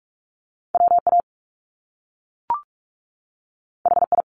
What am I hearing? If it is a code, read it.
Morse code, heard as a keyed tone, about 46 wpm, 700 Hz. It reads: WA HI
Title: US States 45wpm